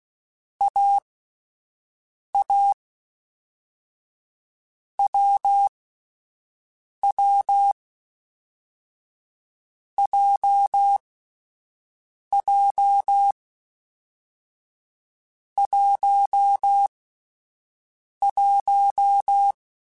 Morse code, heard as a keyed tone, about 16 wpm, 800 Hz. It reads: A A W W J J 1 1
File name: Difficult Combos / Difficult Combos-mp3s / awj1 ndb6 gz7 @10